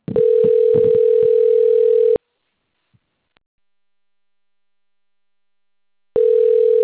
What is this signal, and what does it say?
Ringback tone (North America), 440 + 480 Hz.
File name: rg-1600-unknown- - - .64913